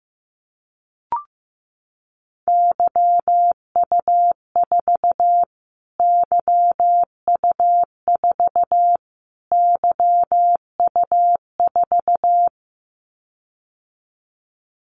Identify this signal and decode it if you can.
Morse code, heard as a keyed tone, about 15 wpm, 700 Hz. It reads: YU4 YU4 YU4
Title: Callsign Prefixes for Top 25 DXCC - Repeated 3x - 15wpm